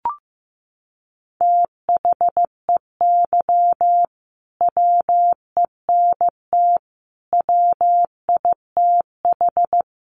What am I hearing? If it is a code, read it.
Morse code, heard as a keyed tone, about 15 wpm, 700 Hz. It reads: THEY WENT WITH
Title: Sentences from Top 800 Words 15wpm